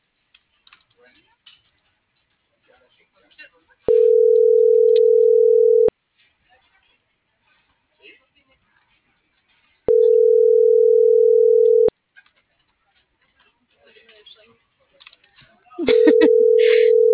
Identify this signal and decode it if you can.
Telephone call progress tone (North America), heard as a telephone network's tone, about 440, 480 Hz, with speech. Ringback tone.